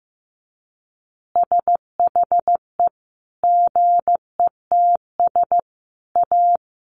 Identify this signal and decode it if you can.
Morse code, heard as a keyed tone, about 15 wpm, 700 Hz. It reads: SHE GETS A
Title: Easy Phrases 15wpm